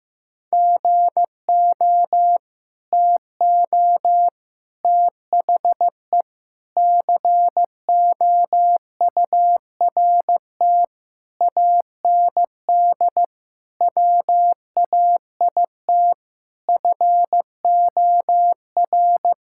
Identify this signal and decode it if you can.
Morse code, heard as a keyed tone, about 15 wpm, 700 Hz. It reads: GO TO THE COURT AND WAIT FOR
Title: Sentences from Top 1600 Words 15wpm